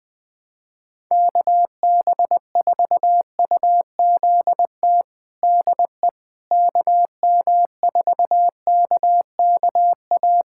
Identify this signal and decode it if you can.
Morse code, heard as a keyed tone, about 20 wpm, 700 Hz. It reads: KB4UZT DE KM4KKA
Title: USA Callsign DE Callsign - 6 Characters - 20wpm